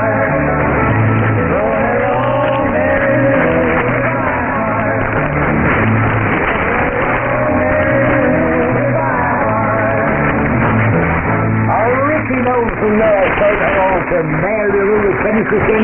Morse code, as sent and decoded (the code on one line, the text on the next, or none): none